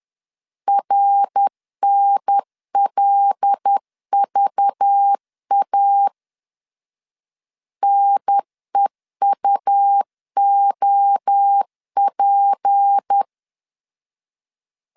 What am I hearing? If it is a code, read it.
Morse code, heard as a keyed tone, about 10 wpm, 800 Hz. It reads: RNLVA NEUOP
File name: CW / DET / 1 6 1500C S B 10wpm 4137